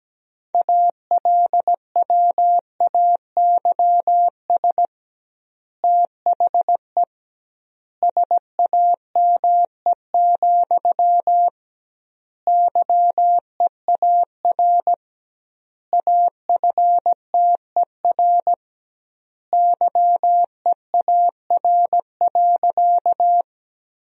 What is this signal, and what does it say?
Morse code, heard as a keyed tone, about 17 wpm, 700 Hz. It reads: ALWAYS THE SAME, YEAR AFTER YEAR.
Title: 2x Word-Spacing - Sentences from Top 300 Words - 17wpm